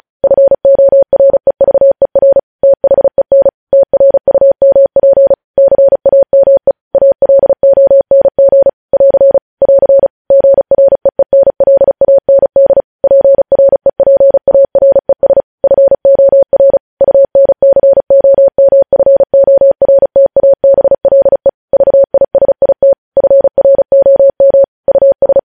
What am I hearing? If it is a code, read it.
Morse code, heard as a keyed tone, about 36 wpm, 550 Hz. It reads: FOREVER THEN TRUMP CAME ALONG <AR> <AR> GREENLAND PREPARES FOR UNCOMFORTABLE VISIT FROM US
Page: News headlines in Morse code at 35 wpm